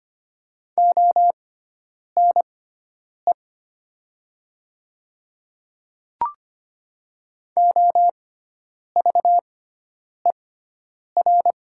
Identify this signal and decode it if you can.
Morse code, heard as a keyed tone, about 25 wpm, spaced as for 9 wpm, 700 Hz. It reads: ONE OVER
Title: 6x Character Spacing - Top 300 Words - 25wpm